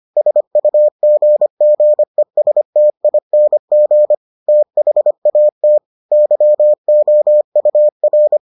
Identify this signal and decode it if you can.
Morse code, heard as a keyed tone, about 25 wpm, 600 Hz. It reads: SUGGESTING THAT YOUR